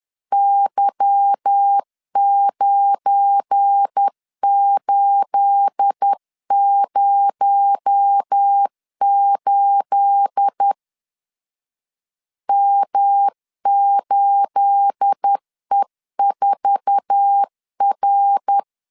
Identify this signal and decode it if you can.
Morse code, heard as a keyed tone, about 10 wpm, 800 Hz. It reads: Y9808 M8E4R
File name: LEZIONI / 9 28 10wpm 2842